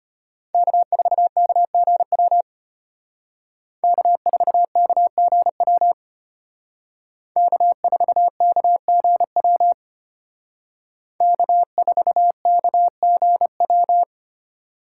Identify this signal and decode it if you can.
Morse code, heard as a keyed tone, about 32 wpm, 700 Hz. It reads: K4KGW K4KGW K4KGW K4KGW